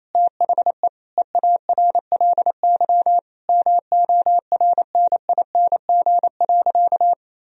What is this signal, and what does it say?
Morse code, heard as a keyed tone, about 28 wpm, 700 Hz. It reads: THE EARLY MORNING.
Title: Sentences from Top 1000 Words - Review Part 3 - 28wpm